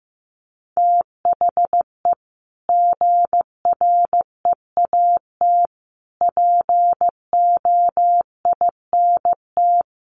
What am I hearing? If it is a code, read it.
Morse code, heard as a keyed tone, about 15 wpm, 700 Hz. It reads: THE GREAT POINT
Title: Sentences from Top 1500 Words 15wpm